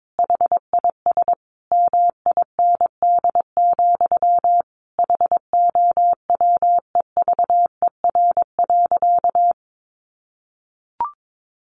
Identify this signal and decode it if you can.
Morse code, heard as a keyed tone, about 22 wpm, 700 Hz. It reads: HIS MIND, HOWEVER.
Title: Sentences from Top 1100 Words 22wpm